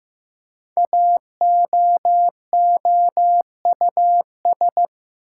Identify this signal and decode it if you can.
Morse code, heard as a keyed tone, about 15 wpm, 700 Hz. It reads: AOOUS
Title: Code Groups - 5 Letters - 15wpm